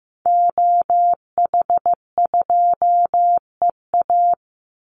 Morse code, heard as a keyed tone, about 15 wpm, 700 Hz. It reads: OH2EA
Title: Call Signs Repeated 2x - 15wpm